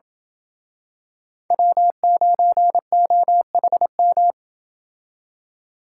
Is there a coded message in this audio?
Yes: Morse code from 1.5 to 4.3 s, heard as a keyed tone, about 28 wpm, 700 Hz. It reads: W9OHM